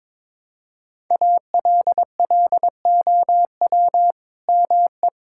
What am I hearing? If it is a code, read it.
Morse code, heard as a keyed tone, about 22 wpm, 700 Hz. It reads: ALLOW ME
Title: Sentences from Top 1000 Words - Review Part 5 - 22wpm